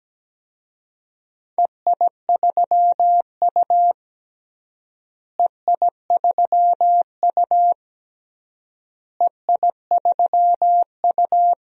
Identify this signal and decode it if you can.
Morse code, heard as a keyed tone, about 17 wpm, 700 Hz. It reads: EI3U EI3U EI3U
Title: DXCC-100 Call Signs - 4 Characters - Repeated 3x - 17wpm